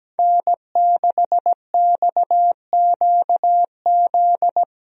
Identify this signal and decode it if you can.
Morse code, heard as a keyed tone, about 17 wpm, 700 Hz. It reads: N6XQZ
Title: USA Call Signs - 5 Characters - Repeated 2x - 17wpm